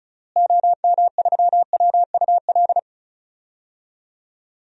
Morse code, heard as a keyed tone, about 35 wpm, 700 Hz. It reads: OM3WUL
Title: DXCC-100 Call Signs - 6 Characters - Repeated 3x - 35wpm